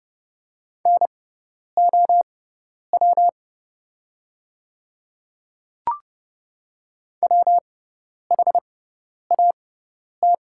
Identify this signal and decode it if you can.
Morse code, heard as a keyed tone, about 30 wpm, spaced as for 10 wpm, 700 Hz. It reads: NOW WHAT
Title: 6x Character Spacing - Top 300 Words - 30wpm